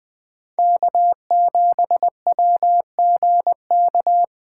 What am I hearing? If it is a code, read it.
Morse code, heard as a keyed tone, about 20 wpm, 700 Hz. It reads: K7WGK
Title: USA Call Signs - 5 Characters - 2x Character Spacing - 20wpm